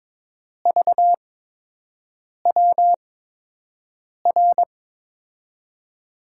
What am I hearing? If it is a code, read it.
Morse code, heard as a keyed tone, about 22 wpm, spaced as for 6 wpm, 700 Hz. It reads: VWR
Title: USA Call Signs - 6 Characters - 8x Character Spacing - 22wpm